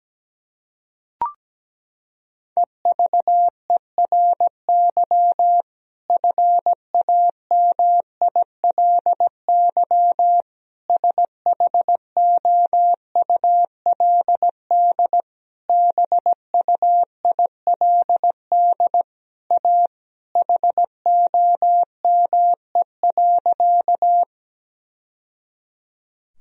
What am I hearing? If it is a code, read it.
Morse code, heard as a keyed tone, about 17 wpm, 700 Hz. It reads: EVERY FAMILY SHOULD BUILD A HOME.
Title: Sentences from Top 1000 Words - Review Part 1 - 17wpm